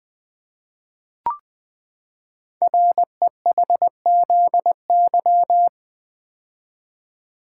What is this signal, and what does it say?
Morse code, heard as a keyed tone, about 20 wpm, 700 Hz. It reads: REHZY
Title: Code Groups - 5 Letters - 20wpm